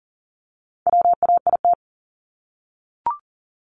Morse code, heard as a keyed tone, about 40 wpm, 700 Hz. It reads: WAIT
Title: Top 500 Words 40wpm